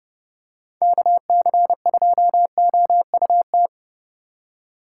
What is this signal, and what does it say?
Morse code, heard as a keyed tone, about 30 wpm, 700 Hz. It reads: KC2OUT
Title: USA Call Signs - 8x Character Spacing - 30wpm